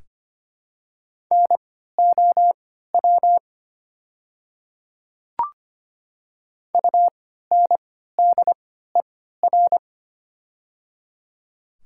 Morse code, heard as a keyed tone, about 25 wpm, spaced as for 14 wpm, 700 Hz. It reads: NOW UNDER
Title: 3x Character Spacing - Top 200 Words - 25wpm